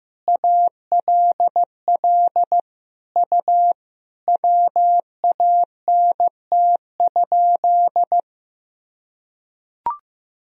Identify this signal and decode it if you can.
Morse code, heard as a keyed tone, about 15 wpm, 700 Hz. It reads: ALL U WANT?
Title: Sentences from Top 200 Words and Abbreviations 15wpm